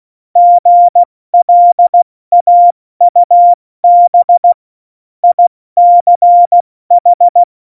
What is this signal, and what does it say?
Morse code, heard as a keyed tone, about 16 wpm, 700 Hz. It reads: GLAUB ICH
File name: Morsekurs / Witze / Witz 13. August 2019, 70 F80